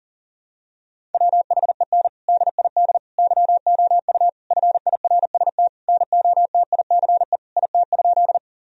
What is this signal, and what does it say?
Morse code, heard as a keyed tone, about 40 wpm, 700 Hz. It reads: WHEN DID YOU FIRST NOTICE IT?
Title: Sentences from Top 1000 Words - Review Part 2 - 40wpm